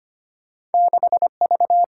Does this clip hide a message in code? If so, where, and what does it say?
Yes: Morse code from 0.7 to 1.8 s, heard as a keyed tone, about 34 wpm, 700 Hz. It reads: 6V